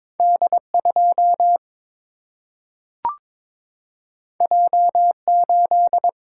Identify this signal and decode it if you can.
Morse code, heard as a keyed tone, about 22 wpm, 700 Hz. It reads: D 2 J 8
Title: Single-Letter Single-Number 22wpm